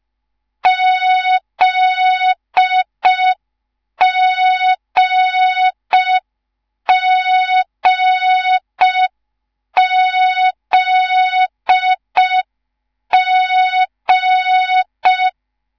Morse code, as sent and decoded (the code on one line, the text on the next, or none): --.. --. --. --.. --.
ZGGZG